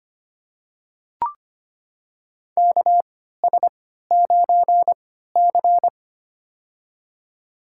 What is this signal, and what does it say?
Morse code, heard as a keyed tone, about 25 wpm, spaced as for 14 wpm, 700 Hz. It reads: KS9C
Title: USA Call Signs - 3x Character Spacing - 25wpm